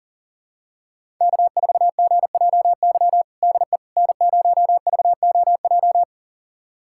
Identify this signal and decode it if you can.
Morse code, heard as a keyed tone, about 37 wpm, 700 Hz. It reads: K4GJY DE N0VOJ
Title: USA Callsign DE Callsign - 5 Characters - 40wpm